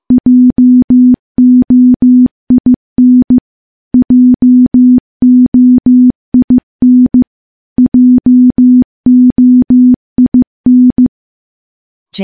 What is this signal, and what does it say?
Morse code, heard as a keyed tone, about 15 wpm, 250 Hz. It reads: JOIN JOIN JOIN
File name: morse-training-15WPM